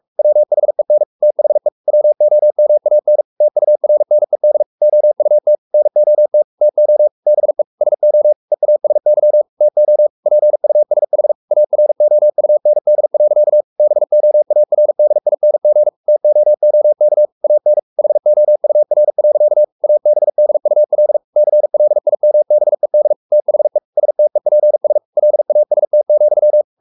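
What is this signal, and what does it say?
Morse code, heard as a keyed tone, about 44 wpm, 600 Hz. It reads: WHEN THE WOMAN TURNED OUT NOT TO BE SO EASY TO PUSH AROUND. BOARDING TOOK AN HOUR. ABDUL CLIMBED THE STEPS LAST,
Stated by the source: Morse code at 43 wpm sending Ken Follett's novel Never in English, from CW med Gnister.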